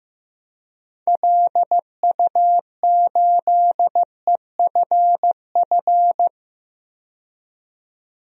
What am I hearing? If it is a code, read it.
Morse code, heard as a keyed tone, about 15 wpm, 700 Hz. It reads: LU8EFF
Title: DXCC-100 Call Signs - 6 Characters - 15wpm